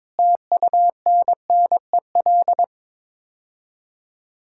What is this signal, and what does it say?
Morse code, heard as a keyed tone, about 22 wpm, 700 Hz. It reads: TUNNEL
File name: practice / 22wpm / Sets of 8 Words - 22wpm